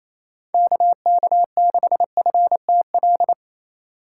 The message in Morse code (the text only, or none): KK6FTL